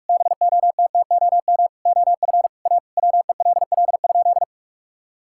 Morse code, heard as a keyed tone, about 46 wpm, 700 Hz. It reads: BOTTOM OF A WELL?
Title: Sentences from Top 1000 Words - Review Part 6 - 45wpm